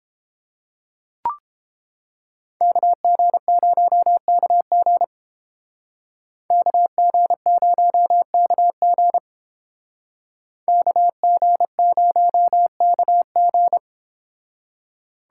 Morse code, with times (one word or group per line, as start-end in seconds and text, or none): 2.6-5.0 KG0KG
6.5-9.2 KG0KG
10.7-13.8 KG0KG